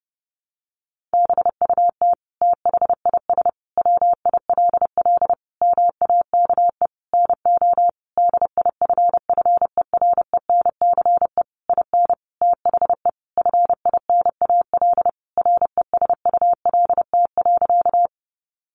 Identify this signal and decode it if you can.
Morse code, heard as a keyed tone, about 30 wpm, 700 Hz. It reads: BUT THIS WILL MAKE NO DIFFERENCE IN THE FINAL RESULT.
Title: Sentences from Top 800 Words 30wpm